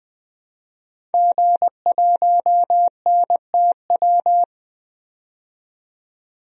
Morse code, heard as a keyed tone, about 21 wpm, 700 Hz. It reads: G1NTW
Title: DXCC-100 Call Signs - 5 Characters - 20wpm